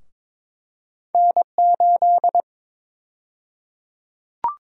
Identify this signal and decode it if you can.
Morse code, heard as a keyed tone, about 22 wpm, 700 Hz. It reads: N8
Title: USA Call Sign Prefixes - 22wpm